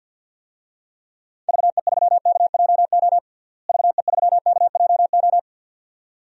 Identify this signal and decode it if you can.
Morse code, heard as a keyed tone, about 49 wpm, 700 Hz. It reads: VE3KJO VE3KJO
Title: DXCC-100 Call Signs - 6 Characters - Repeated 2x - 50wpm